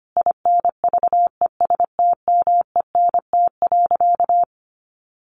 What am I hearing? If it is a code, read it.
Morse code, heard as a keyed tone, about 25 wpm, 700 Hz. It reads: INVESTMENT.
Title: Sentences from Top 700 Words 25wpm